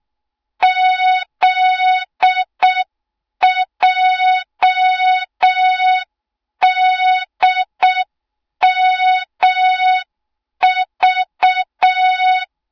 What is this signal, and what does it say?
Morse code, heard as a keyed tone, about 6 wpm, 3700 Hz. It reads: ZJDMV
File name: Morze-HA2MN / hang / 20 GYAKORLAT